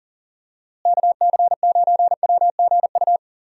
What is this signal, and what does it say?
Morse code, heard as a keyed tone, about 40 wpm, 700 Hz. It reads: KC9WGU
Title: USA Call Signs - 6 Characters - Repeated 3x - 40wpm